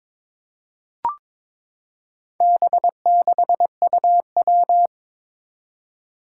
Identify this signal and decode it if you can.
Morse code, heard as a keyed tone, about 22 wpm, 700 Hz. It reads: B6UW
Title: Four Challenging Letter-Number Combinations 22wpm